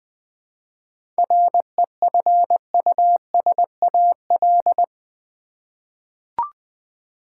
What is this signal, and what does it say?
Morse code, heard as a keyed tone, about 20 wpm, 700 Hz. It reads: REFUSAL
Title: 7 Letter Words - First 3 Letters Given - 20wpm